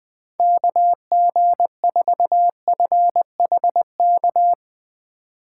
Morse code, heard as a keyed tone, about 20 wpm, 700 Hz. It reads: KG4FHK
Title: Speed Racing - USA 2x3 Call Signs 17wpm